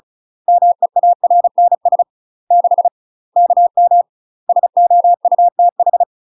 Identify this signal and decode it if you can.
Morse code, heard as a keyed tone, about 34 wpm, 700 Hz. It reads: MEARNS 6 KM SOUTH
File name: UK MASTER QSOs TRAINING / 35WPM / UK MASTER QSOs 0024